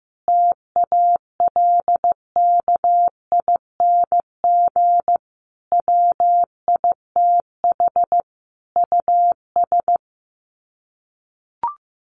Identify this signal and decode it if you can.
Morse code, heard as a keyed tone, about 15 wpm, 700 Hz. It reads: TALKING WITH US
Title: Sets of 5 Words Encore 15wpm